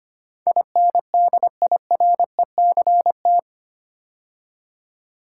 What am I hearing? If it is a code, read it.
Morse code, heard as a keyed tone, about 25 wpm, 690 Hz. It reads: INDIRECT
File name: 8 Letter Words - First 3 Letters Given - 25wpm